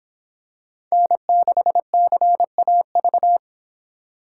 Morse code, heard as a keyed tone, about 23 wpm, 700 Hz. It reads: N6CAV